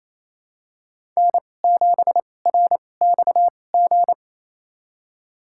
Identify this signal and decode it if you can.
Morse code, heard as a keyed tone, about 28 wpm, spaced as for 20 wpm, 700 Hz. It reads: N7RXG
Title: USA Call Signs - 5 Characters - 2x Character Spacing - 28wpm